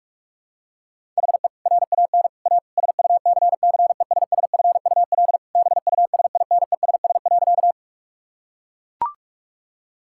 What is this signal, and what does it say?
Morse code, heard as a keyed tone, about 44 wpm, 700 Hz. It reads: HE RAN A SUCCESSFUL BUSINESS.
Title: Sentences from Top 1000 Words - Review Part 6 - 45wpm